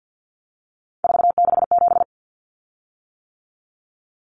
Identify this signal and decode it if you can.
Morse code, heard as a keyed tone, about 50 wpm, 700 Hz. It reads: 467
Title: Common Cut Numbers - 3-Digit 50wpm